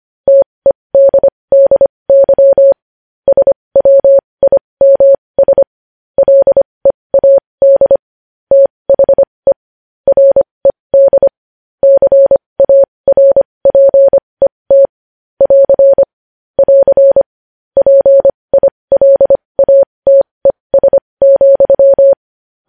Morse code, heard as a keyed tone, about 25 wpm, spaced as for 20 wpm, 550 Hz. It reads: TEDDY SWIMS LEAD THE RED CARPET <AR> <AR> PILATES,
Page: news-headlines-in-morse-code-at-20-wpm